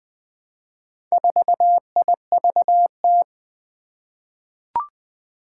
Morse code, heard as a keyed tone, about 20 wpm, 700 Hz. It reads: 4IVT
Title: Four Challenging Letter-Number Combinations 20wpm